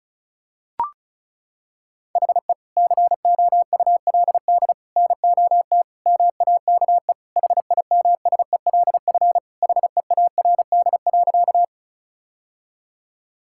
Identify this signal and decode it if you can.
Morse code, heard as a keyed tone, about 36 wpm, 700 Hz. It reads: HE COULD NOT MAKE HIMSELF HEARD.
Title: Sentences from Top 1600 Words 35wpm